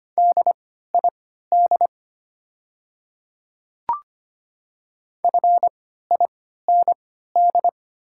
Morse code, heard as a keyed tone, about 25 wpm, spaced as for 14 wpm, 700 Hz. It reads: DID FIND
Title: 3x Character Spacing - Top 100 Words - 25wpm